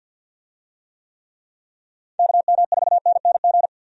Morse code, heard as a keyed tone, about 53 wpm, 700 Hz. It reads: KM4NNG
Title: USA Call Signs - 6 Characters - Repeated 2x - 50wpm